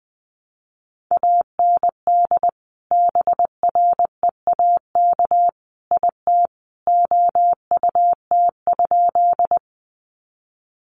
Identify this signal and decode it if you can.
Morse code, heard as a keyed tone, about 19 wpm, 700 Hz. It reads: AND BREAK IT OUT?
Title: Sentences from Top 900 Words 20wpm